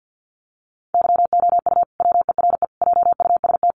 Morse code, heard as a keyed tone, about 50 wpm, 700 Hz. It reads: YOU WERE JUST
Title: Sentences from Top 1100 Words 50wpm